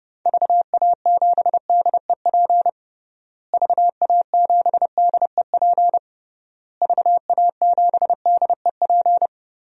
Morse code, heard as a keyed tone, about 30 wpm, 700 Hz. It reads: VA7DEP VA7DEP VA7DEP
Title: DXCC-100 Call Signs - 6 Characters - Repeated 3x - 30wpm